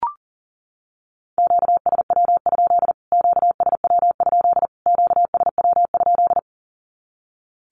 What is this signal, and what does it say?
Morse code, heard as a keyed tone, about 40 wpm, 700 Hz. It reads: QSW? QSW? QSW?
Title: Q-Codes Repeated 3x 40wpm